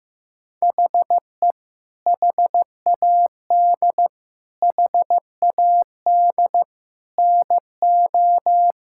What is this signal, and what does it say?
Morse code, heard as a keyed tone, about 15 wpm, 700 Hz. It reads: HE HAD HAD NO
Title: Sentences from Top 1000 Words - Review Part 6 - 15wpm